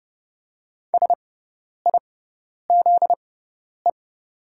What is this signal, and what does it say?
Morse code, heard as a keyed tone, about 30 wpm, spaced as for 10 wpm, 700 Hz. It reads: SIZE
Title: 6x Character Spacing - Top 500 Words - 30wpm